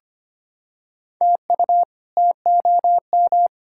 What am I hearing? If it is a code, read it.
Morse code, heard as a keyed tone, about 25 wpm, 700 Hz. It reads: TU TOM
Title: K1USN SST QSO Practice - 25wpm